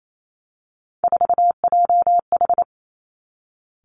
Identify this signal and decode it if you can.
Morse code, heard as a keyed tone, about 28 wpm, 690 Hz. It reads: 4JH
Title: Three Challenging Letter-Number Combinations 28wpm